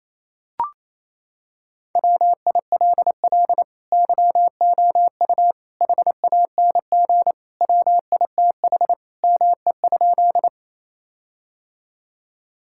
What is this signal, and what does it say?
Morse code, heard as a keyed tone, about 28 wpm, 700 Hz. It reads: WILL YOU HANG WITH ME?
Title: Sentences from Top 1600 Words 28wpm